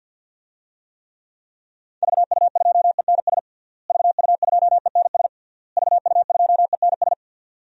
Morse code, heard as a keyed tone, about 46 wpm, 700 Hz. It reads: VU2ENS VU2ENS VU2ENS
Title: DXCC-100 Call Signs - 6 Characters - Repeated 3x - 50wpm